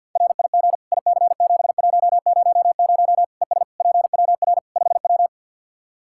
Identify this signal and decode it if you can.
Morse code, heard as a keyed tone, about 48 wpm, 700 Hz. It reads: RIG IC7100 ES PWR 5W
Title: Practice QSO - Exchange 2 - Info - 50wpm